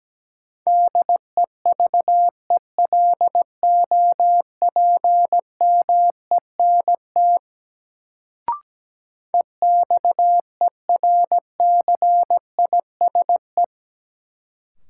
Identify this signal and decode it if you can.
Morse code, heard as a keyed tone, about 17 wpm, 700 Hz. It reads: DEVELOPMENT EXERCISE
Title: Sentences from Top 1000 Words - Review Part 4 - New Word Drill - 17wpm